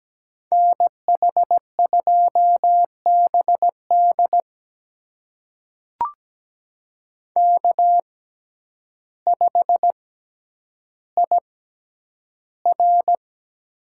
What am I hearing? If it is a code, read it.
Morse code, heard as a keyed tone, about 17 wpm, spaced as for 9 wpm, 700 Hz. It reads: NH2BD K5IR
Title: USA Call Signs - 6x Character Spacing - 17wpm